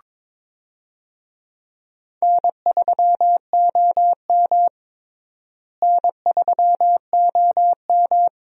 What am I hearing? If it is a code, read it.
Morse code, heard as a keyed tone, about 22 wpm, 700 Hz. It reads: N3OM N3OM
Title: USA Call Signs - 4 Characters - Repeated 3x - 22wpm